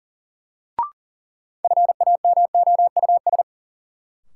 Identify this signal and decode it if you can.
Morse code, heard as a keyed tone, about 40 wpm, 700 Hz. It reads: FAMOUS
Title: Sentences from Top 1500 Words - New Word Drill Alt - 40wpm